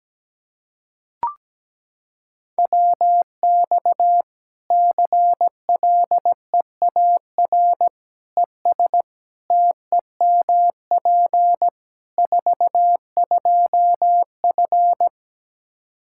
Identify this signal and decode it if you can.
Morse code, heard as a keyed tone, about 17 wpm, 700 Hz. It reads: WX CLEAR ES TEMP 42F
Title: Practice QSO - Exchange 2 - Info - 17wpm